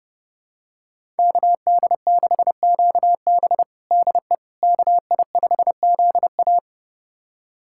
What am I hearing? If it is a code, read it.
Morse code, heard as a keyed tone, about 30 wpm, 700 Hz. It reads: KD6QB DE KI5ZA
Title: USA Callsign DE Callsign - 5 Characters - 30wpm